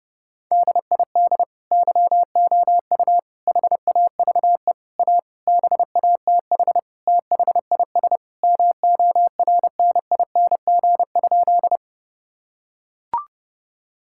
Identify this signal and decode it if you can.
Morse code, heard as a keyed tone, about 30 wpm, 700 Hz. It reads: DID YOU HAVE A BATH THIS MORNING?
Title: Sentences from Top 1600 Words 30wpm